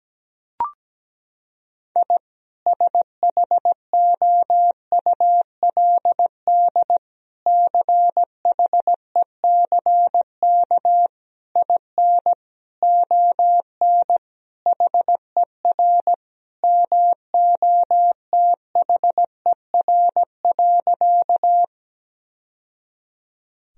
Morse code, heard as a keyed tone, about 17 wpm, 700 Hz. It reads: I SHOULD CHECK IN ON HER MOTHER.
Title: Sentences from Top 1000 Words - Review Part 3 - 17wpm